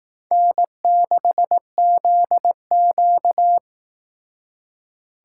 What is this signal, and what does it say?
Morse code, heard as a keyed tone, about 18 wpm, 700 Hz. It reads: N6ZQ
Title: Speed Racing - USA 1x2 Call Signs 15wpm